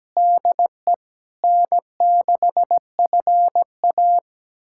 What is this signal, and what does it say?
Morse code, heard as a keyed tone, about 17 wpm, 700 Hz. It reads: DE N6FA
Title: USA Callsign DE Callsign - 4 Characters - 17wpm